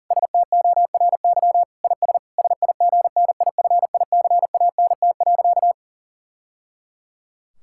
Morse code, heard as a keyed tone, about 40 wpm, 700 Hz. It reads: STORY IS SIGNIFICANT.
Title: Sentences from Top 1000 Words - Review Part 6 - 40wpm